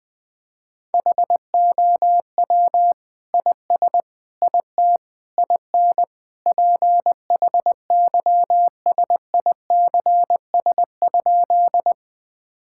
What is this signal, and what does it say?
Morse code, heard as a keyed tone, about 20 wpm, 700 Hz. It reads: HOW IS IT IN PHYSICS?
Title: Sentences from Top 1000 Words - Review Part 5 - 20wpm